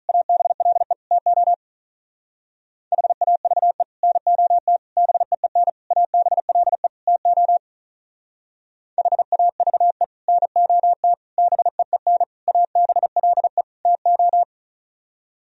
Morse code, heard as a keyed tone, about 46 wpm, 700 Hz. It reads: ABLE TO HAVE NOT BEEN ABLE TO HAVE NOT BEEN ABLE TO